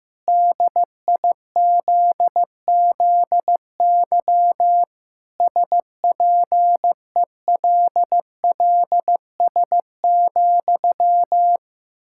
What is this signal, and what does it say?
Morse code, heard as a keyed tone, about 15 wpm, 700 Hz. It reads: DIZZY SPELLS,
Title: Pangram Challenge v2 - 15wpm